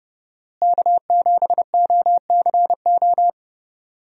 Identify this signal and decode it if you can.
Morse code, heard as a keyed tone, about 30 wpm, 700 Hz. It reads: K7OCO
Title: USA Call Signs - 5 Characters - Repeated 3x - 30wpm